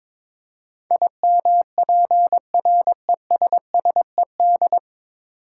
Morse code, heard as a keyed tone, about 22 wpm, 700 Hz. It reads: IMPRESSED
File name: 9 Letter Words - First 3 Letters Given - 22wpm